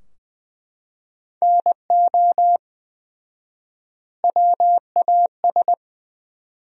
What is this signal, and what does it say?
Morse code, heard as a keyed tone, about 21 wpm, 700 Hz. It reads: NO WAS
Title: 4x Word-Spacing - Sentences from Top 100 Words - 20wpm